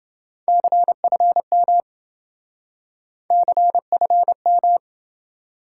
Morse code, heard as a keyed tone, about 28 wpm, 700 Hz. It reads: CFM CFM